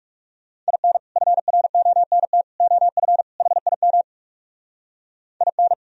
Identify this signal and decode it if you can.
Morse code, heard as a keyed tone, about 38 wpm, 700 Hz. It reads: IN FRONT OF HIM IN